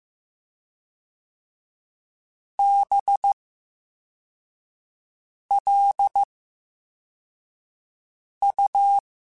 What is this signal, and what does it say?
Morse code, heard as a keyed tone, about 15 wpm, 800 Hz. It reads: B L U
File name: Special INGs / Special INGs-mp3s / Slow ing @13